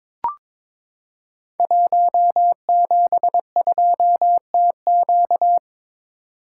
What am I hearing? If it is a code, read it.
Morse code, heard as a keyed tone, about 23 wpm, 700 Hz. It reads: 172TQ
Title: Code Groups - 5 Letter:Numbers - 22wpm